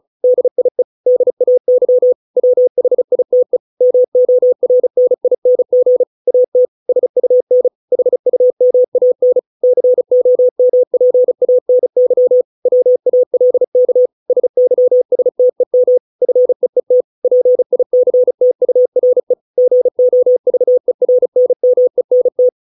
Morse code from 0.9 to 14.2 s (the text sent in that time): DAY WHITE MORNING AT SUN HUMAN COMPANY WALK